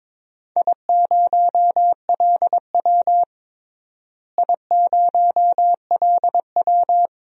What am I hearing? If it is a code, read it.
Morse code, heard as a keyed tone, about 22 wpm, 700 Hz. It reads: I0LW I0LW
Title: DXCC-100 Call Signs - 4 Characters - Repeated 3x - 22wpm